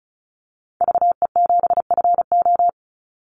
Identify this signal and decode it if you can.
Morse code, heard as a keyed tone, about 34 wpm, 700 Hz. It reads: VE7FO
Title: Call Signs Repeated 3x - 35wpm